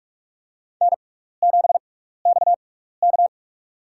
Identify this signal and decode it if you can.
Morse code, heard as a keyed tone, about 46 wpm, spaced as for 15 wpm, 700 Hz. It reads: N7XK
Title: USA Call Signs - 6x Character Spacing - 45wpm